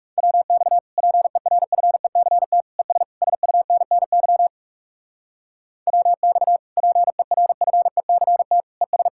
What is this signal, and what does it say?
Morse code, heard as a keyed tone, about 38 wpm, 700 Hz. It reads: WX PERFECT ES SUNNY WX PERFECT ES